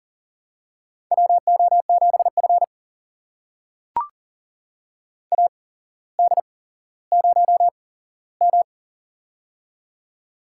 Morse code, heard as a keyed tone, about 38 wpm, spaced as for 17 wpm, 700 Hz. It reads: WO7F AD0M
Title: USA Call Signs - 4 Characters - 8x Character Spacing - 40wpm